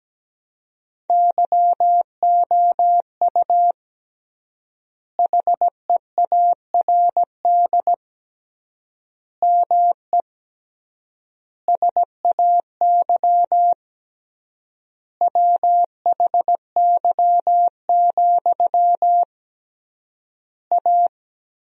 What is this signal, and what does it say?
Morse code, heard as a keyed tone, about 17 wpm, 700 Hz. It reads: YOU HEARD ME SAY WHY, A
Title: 3x Word-Spacing - Sentences from Top 500 Words - 17wpm